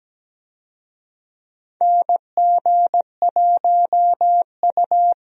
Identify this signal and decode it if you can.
Morse code, heard as a keyed tone, about 17 wpm, 700 Hz. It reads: NG1U
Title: USA Call Signs - 4 Characters - 17wpm